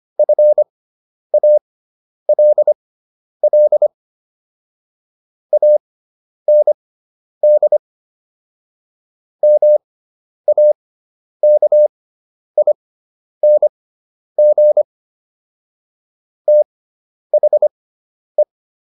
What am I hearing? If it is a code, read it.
Morse code, heard as a keyed tone, about 25 wpm, spaced as for 10 wpm, 600 Hz. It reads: FALL AND MAKING THE